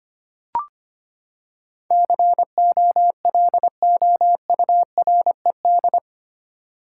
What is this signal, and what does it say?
Morse code, heard as a keyed tone, about 25 wpm, 700 Hz. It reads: COLOURED
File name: Eight Letter Words - 25wpm